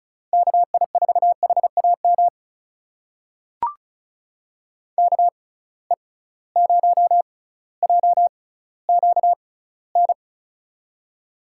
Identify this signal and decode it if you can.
Morse code, heard as a keyed tone, about 36 wpm, spaced as for 18 wpm, 700 Hz. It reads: KI4HAM KE0JQN
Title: USA Call Signs - 6x Character Spacing - 35wpm